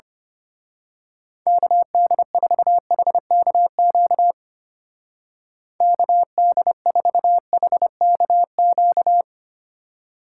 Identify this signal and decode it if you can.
Morse code, heard as a keyed tone, about 29 wpm, 700 Hz. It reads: KD4HKQ KD4HKQ
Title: Speed Racing - USA 2x3 Call Signs 25wpm